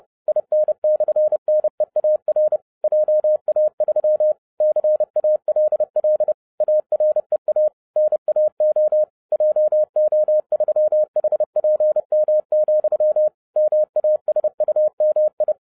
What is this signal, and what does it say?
Morse code, heard as a keyed tone, about 30 wpm, 610 Hz. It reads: IN/NEAR JA3 CALL AREA NAO JO3HPM, MASUMI